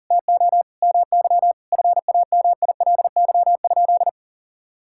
Morse code, heard as a keyed tone, about 37 wpm, 690 Hz. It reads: TO MY FAMILY?
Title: Sentences from Top 1000 Words - Review Part 5 - 40wpm